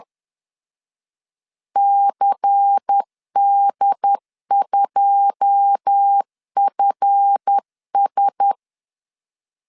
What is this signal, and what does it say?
Morse code, heard as a keyed tone, about 10 wpm, 800 Hz. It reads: CD2FS
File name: LEZIONI / 10 SIGNS / 10D COMA 10wpm 3013